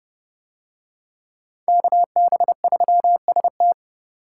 Morse code, heard as a keyed tone, about 28 wpm, 700 Hz. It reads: KB3ST